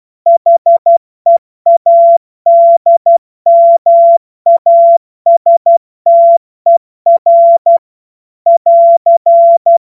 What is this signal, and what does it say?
Morse code, heard as a keyed tone, about 12 wpm, 680 Hz. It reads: HEADMASTER <AR>